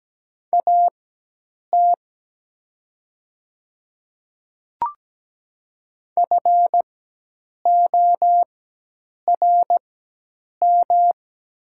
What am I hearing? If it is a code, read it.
Morse code, heard as a keyed tone, about 17 wpm, spaced as for 8 wpm, 700 Hz. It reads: AT FORM
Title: 4x Character Spacing - Top 200 Words - 17wpm